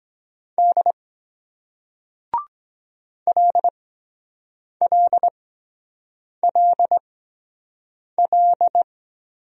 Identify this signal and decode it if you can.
Morse code, heard as a keyed tone, about 23 wpm, spaced as for 7 wpm, 700 Hz. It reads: D LLLL